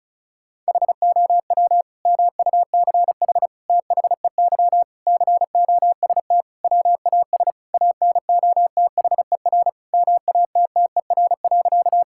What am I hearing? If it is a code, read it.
Morse code, heard as a keyed tone, about 34 wpm, 700 Hz. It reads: HOW MUCH THEY COST WAS ANOTHER MATTER.
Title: Sentences from Top 1000 Words - Review Part 5 - 35wpm